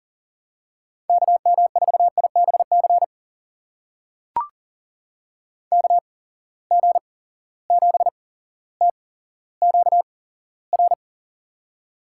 Morse code, heard as a keyed tone, about 38 wpm, spaced as for 17 wpm, 700 Hz. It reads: KM4IBC KG7TQR
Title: USA Call Signs - 6 Characters - 8x Character Spacing - 40wpm